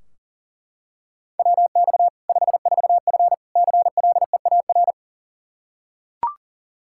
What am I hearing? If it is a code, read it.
Morse code, heard as a keyed tone, about 40 wpm, 700 Hz. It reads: WX 54F CLEAR
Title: K1USN SST QSO Practice 40wpm